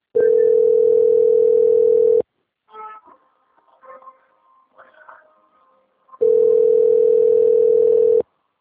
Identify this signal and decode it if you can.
Ringback tone (North America), 440 + 480 Hz.